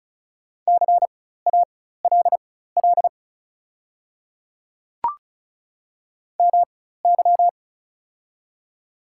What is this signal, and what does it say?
Morse code, heard as a keyed tone, about 36 wpm, spaced as for 16 wpm, 700 Hz. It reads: CALL MY
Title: 4x Character Spacing - Top 100 Words - 35wpm